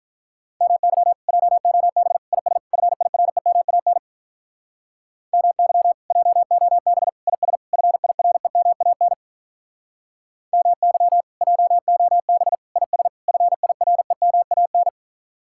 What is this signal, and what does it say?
Morse code, heard as a keyed tone, about 43 wpm, 700 Hz. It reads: MY JOB IS FIREMAN MY JOB IS FIREMAN MY JOB IS FIREMAN
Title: Speed-Racing Common QSO Phrases - 35wpm